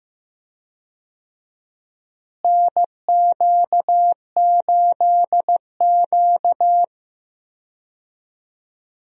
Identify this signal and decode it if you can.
Morse code, heard as a keyed tone, about 15 wpm, 700 Hz. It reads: NQ8Q
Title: USA Call Signs - 4 Characters - Repeated 2x - 15wpm